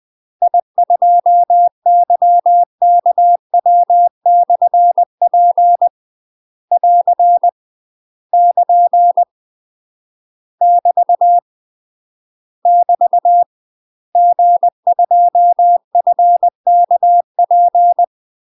Morse code, heard as a keyed tone, about 20 wpm, 700 Hz. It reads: I2YKW/P <AR> <KN> <BT> <BT> G2FKP
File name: UK MASTER QSOs TRAINING / 20WPM / UK MASTER QSOs 0037